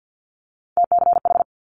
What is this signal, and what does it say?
Morse code, heard as a keyed tone, about 50 wpm, 700 Hz. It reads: TCH
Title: Multi-letter Phonemes Quick Fire 50wpm